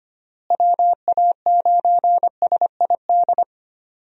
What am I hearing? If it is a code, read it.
Morse code, heard as a keyed tone, about 25 wpm, 700 Hz. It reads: WA9SID
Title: USA Call Signs - 6 Characters - Repeated 2x - 25wpm